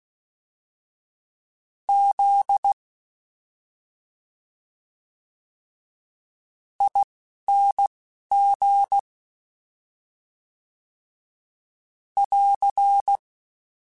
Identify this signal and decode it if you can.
Morse code, heard as a keyed tone, about 16 wpm, spaced as for 5 wpm, 800 Hz. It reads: ZING <AR>
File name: Special INGs-mp3s / Special INGs-mp3s / Slow ing @10